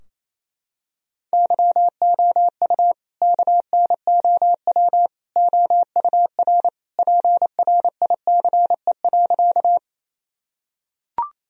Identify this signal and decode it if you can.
Morse code, heard as a keyed tone, about 28 wpm, 700 Hz. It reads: YOU KNOW OUR PRICE.
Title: Sentences from Top 1000 Words - Review Part 4 - 28wpm